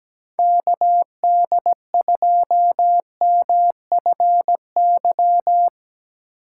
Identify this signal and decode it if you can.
Morse code, heard as a keyed tone, about 17 wpm, 700 Hz. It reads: KD2MFY
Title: USA Call Signs - 6 Characters - 3x Character Spacing - 17wpm